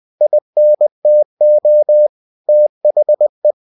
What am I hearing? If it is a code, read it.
Morse code, heard as a keyed tone, about 20 wpm, 600 Hz. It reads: INTO THE
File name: cw-books / winnie-the-pooh / winnie-20wpm-page-0011